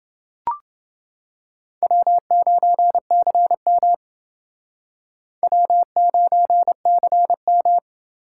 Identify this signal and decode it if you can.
Morse code, heard as a keyed tone, about 29 wpm, 700 Hz. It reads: W9CM W9CM